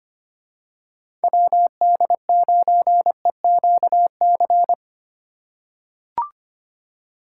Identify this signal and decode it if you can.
Morse code, heard as a keyed tone, about 25 wpm, 700 Hz. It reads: WD9EQC
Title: USA Call Signs - 6 Characters - 4x Character Spacing - 25wpm